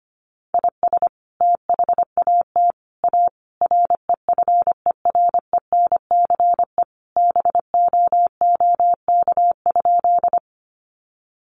Morse code, heard as a keyed tone, about 25 wpm, 700 Hz. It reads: IS THAT A REFERENCE BOOK?
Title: Sentences from Top 1200 Words 25wpm